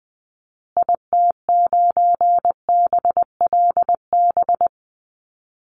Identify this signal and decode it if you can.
Morse code, heard as a keyed tone, about 20 wpm, 700 Hz. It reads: IT9BLB
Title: Call Signs Repeated 3x - 20wpm